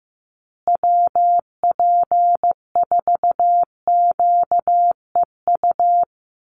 Morse code, heard as a keyed tone, about 15 wpm, 700 Hz. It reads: WP4QEU
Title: USA Call Signs Repeated 3x 15wpm